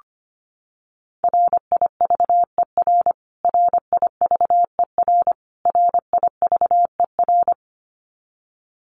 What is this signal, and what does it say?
Morse code, heard as a keyed tone, about 25 wpm, 700 Hz. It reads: RIVER RIVER RIVER
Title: Top 500 Words Repeated 25wpm